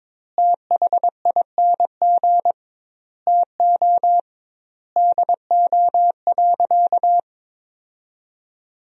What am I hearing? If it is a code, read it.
Morse code, heard as a keyed tone, about 22 wpm, 700 Hz. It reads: THING TO DO.
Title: 2x Word-Spacing - Sentences from Top 100 Words - 22wpm